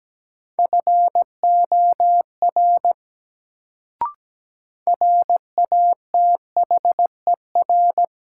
Morse code, heard as a keyed tone, about 17 wpm, 700 Hz. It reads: FOR RATHER
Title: Sets of 2 Words - New Word Drill - 17wpm